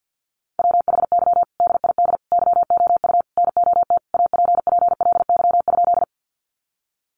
Morse code, heard as a keyed tone, about 50 wpm, 700 Hz. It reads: WHY DID YOU NOT APPLY?
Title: Sentences from Top 1200 Words 50wpm